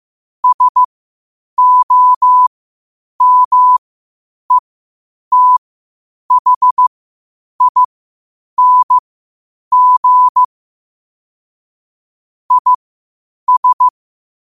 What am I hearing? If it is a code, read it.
Morse code, heard as a keyed tone, about 15 wpm, spaced as for 8 wpm, 1000 Hz. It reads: SOMETHING IS
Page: New quotes every day in morse code at 8 Words per minute.